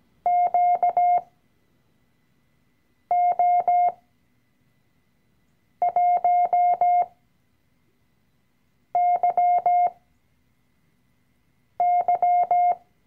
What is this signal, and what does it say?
Morse code, heard as a keyed tone, about 17 wpm, spaced as for 4 wpm, 710 Hz. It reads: QO1YY
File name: CW-Online-Kurs / Anfänger / Lektion 012